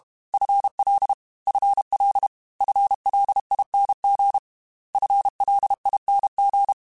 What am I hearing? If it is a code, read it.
Morse code, heard as a keyed tone, about 32 wpm, 800 Hz. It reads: FL FL FLING FLING
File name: FL / FL mp3s / FL Prefixes 3x2 @30